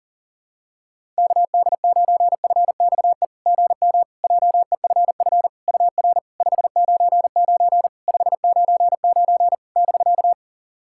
Morse code, heard as a keyed tone, about 40 wpm, 690 Hz. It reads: KD9FXE GM JEFF UR 599 599 <BK>
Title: SOTA QSO Practice 40wpm